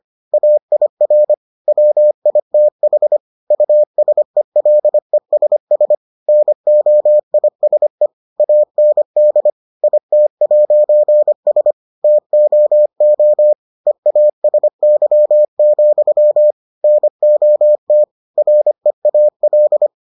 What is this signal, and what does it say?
Morse code, heard as a keyed tone, about 25 wpm, 600 Hz. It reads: AIR WITH USELESS NOISE AND IT'S TOO EASY, NOT REAL